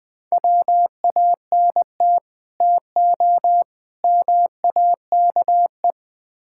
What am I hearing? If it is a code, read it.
Morse code, heard as a keyed tone, about 20 wpm, 700 Hz. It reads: WANT TO MAKE
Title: Build A Phrase - 5-Word Phrases - 20wpm